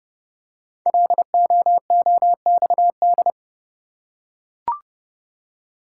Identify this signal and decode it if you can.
Morse code, heard as a keyed tone, about 30 wpm, 700 Hz. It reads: LOOXD
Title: Code Groups - 5 Letter:Numbers - 30wpm